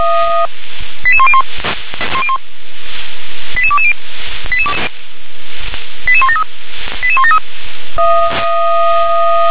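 Five-tone selective call (ZVEI), heard as quick runs of five tones, tones of about 70 ms. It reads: 88101 88220 88173 88173
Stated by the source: ZVEI five-tone call 88051, received by SDR.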